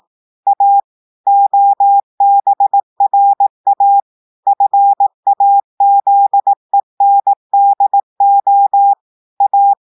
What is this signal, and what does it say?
Morse code, heard as a keyed tone, about 18 wpm, 800 Hz. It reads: A OBRA FAZENDO A
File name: audios / portugais / portugais texte en clair 1 vitesse 18 mots minutes